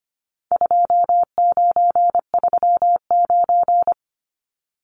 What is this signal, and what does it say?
Morse code, heard as a keyed tone, about 25 wpm, 700 Hz. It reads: 2939
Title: Numbers - 4-Digit 25wpm